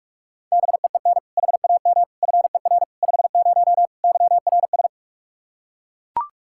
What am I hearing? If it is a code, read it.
Morse code, heard as a keyed tone, about 41 wpm, 700 Hz. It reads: BEEN HAM FER 50 YRS
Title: Practice QSO - Exchange 3 - More Info - 45wpm